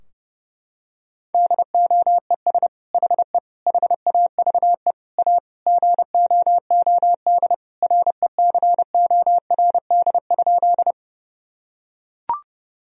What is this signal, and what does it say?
Morse code, heard as a keyed tone, about 32 wpm, 700 Hz. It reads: DOES HE HAVE A GOOD RECORD?
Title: Sentences from Top 1000 Words - Review Part 3 - 30wpm